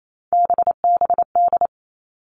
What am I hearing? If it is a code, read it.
Morse code, heard as a keyed tone, about 28 wpm, 700 Hz. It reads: BBD
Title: All Cut Numbers - 3-Digit 28wpm